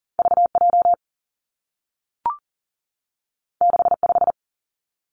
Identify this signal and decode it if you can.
Morse code, heard as a keyed tone, about 40 wpm, 700 Hz. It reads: V J 6 5
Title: Two Challenging Letter-Number Combinations 40wpm